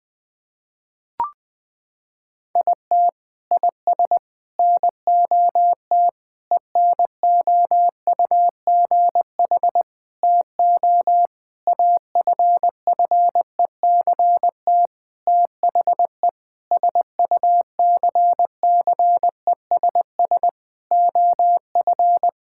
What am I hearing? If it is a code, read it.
Morse code, heard as a keyed tone, about 20 wpm, 700 Hz. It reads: IT IS NOT ENOUGH TO AFFECT THE SUCCESS OF
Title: Sentences from Top 1000 Words - Review Part 6 - 20wpm